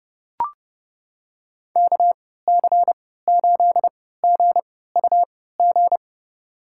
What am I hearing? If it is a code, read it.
Morse code, heard as a keyed tone, about 32 wpm, spaced as for 17 wpm, 700 Hz. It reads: KC8GUG
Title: USA Call Signs - 3x Character Spacing - 30wpm